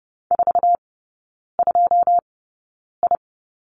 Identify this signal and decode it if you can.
Morse code, heard as a keyed tone, about 30 wpm, spaced as for 9 wpm, 700 Hz. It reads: 42I
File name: 081 Lesson 14e - Number 2 - Three Characters 30wpm